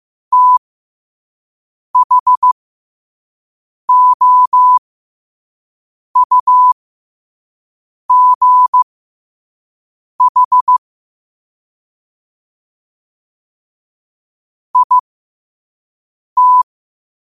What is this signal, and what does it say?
Morse code, heard as a keyed tone, about 15 wpm, spaced as for 5 wpm, 1000 Hz. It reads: THOUGH IT
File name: 5wpm